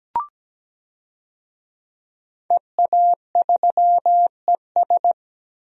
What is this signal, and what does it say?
Morse code, heard as a keyed tone, about 17 wpm, 700 Hz. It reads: EA3ES
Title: DXCC-100 Call Signs - 5 Characters - Repeated 2x - 17wpm